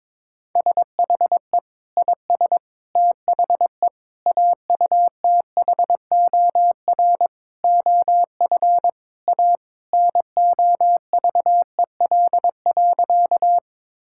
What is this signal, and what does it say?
Morse code, heard as a keyed tone, about 22 wpm, 700 Hz. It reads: SHE IS THE AUTHOR OF A NOVEL.
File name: practice / 22wpm / Sentences with 4-Word Phrases - 22wpm